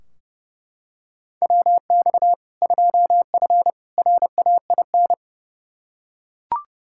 Morse code, heard as a keyed tone, about 30 wpm, 700 Hz. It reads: WX 2F RAIN
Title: K1USN SST QSO Practice 30wpm